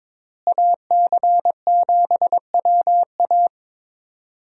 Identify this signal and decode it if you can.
Morse code, heard as a keyed tone, about 22 wpm, 700 Hz. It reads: AC7WA
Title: USA Call Signs - 5 Characters - Repeated 2x - 22wpm